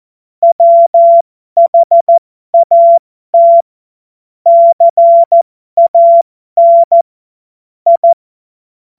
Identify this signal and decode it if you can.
Morse code, heard as a keyed tone, about 14 wpm, spaced as for 12 wpm, 680 Hz. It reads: WHAT CAN I
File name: Morsekurs / Witze